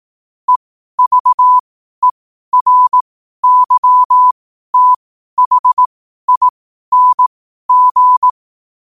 Morse code, heard as a keyed tone, about 18 wpm, spaced as for 12 wpm, 1000 Hz. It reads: EVERYTHING
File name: qotdfiles / 12wpm